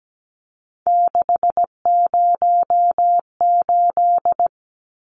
Morse code, heard as a keyed tone, about 17 wpm, 700 Hz. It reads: 608
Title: Numbers - 3-Digit 17wpm